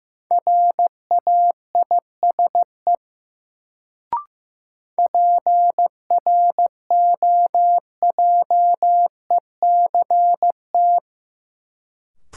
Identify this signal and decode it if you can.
Morse code, heard as a keyed tone, about 15 wpm, 700 Hz. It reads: RAISE PROJECT
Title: Sentences from Top 1400 Words - New Word Drill - 15wpm